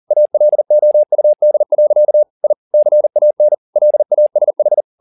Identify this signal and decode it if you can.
Morse code, heard as a keyed tone, about 40 wpm, 600 Hz. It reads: ALOUD. I CAN LASH
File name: the old man and the sea 2 40wpm